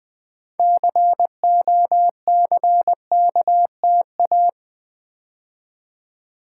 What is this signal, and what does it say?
Morse code, heard as a keyed tone, about 20 wpm, 700 Hz. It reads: COCKTA
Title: Build A Word - 3 Letters At A Time - 8-Letter Words - 20wpm